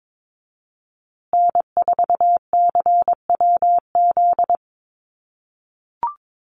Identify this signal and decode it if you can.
Morse code, heard as a keyed tone, about 22 wpm, 700 Hz. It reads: N4CWZ
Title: USA Call Signs Repeated 3x 22wpm